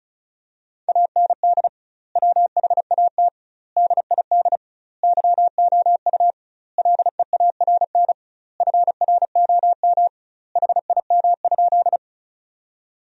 Morse code, heard as a keyed tone, about 36 wpm, 700 Hz. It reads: AND WHAT DID YOU LEARN FROM HIM?
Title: 2x Word-Spacing - Sentences from Top 300 Words - 35wpm